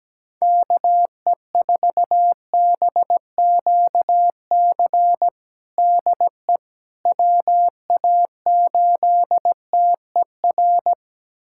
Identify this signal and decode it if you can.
Morse code, heard as a keyed tone, about 17 wpm, 700 Hz. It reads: KE4BQC DE WA8TER
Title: USA Callsign DE Callsign 17wpm